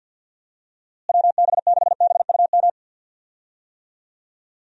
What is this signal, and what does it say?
Morse code, heard as a keyed tone, about 50 wpm, 700 Hz. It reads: WB6BUM
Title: USA Call Signs - 6 Characters - Repeated 2x - 50wpm